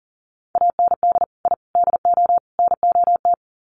Morse code, heard as a keyed tone, about 40 wpm, 700 Hz. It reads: AND I DO NOT
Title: Sets of 4 Words 40wpm